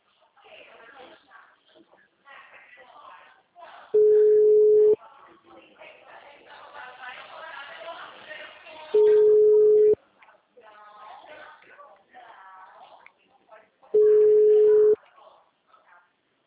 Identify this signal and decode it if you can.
Ringback tone (Europe, 425 Hz), with speech.